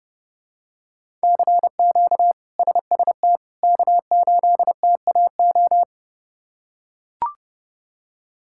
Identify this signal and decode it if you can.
Morse code, heard as a keyed tone, about 30 wpm, 700 Hz. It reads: CQ SST K8TAO
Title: K1USN SST QSO Practice - 30wpm